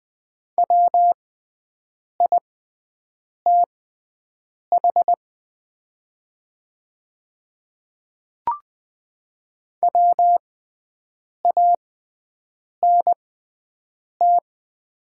Morse code, heard as a keyed tone, about 20 wpm, spaced as for 7 wpm, 700 Hz. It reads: WITH WANT
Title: 6x Character Spacing - Top 200 Words - 20wpm